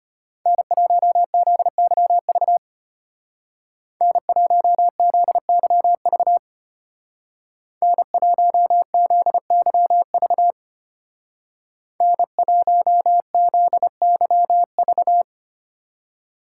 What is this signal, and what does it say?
Morse code, heard as a keyed tone, about 32 wpm, 700 Hz. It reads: N1ZYV N1ZYV N1ZYV N1ZYV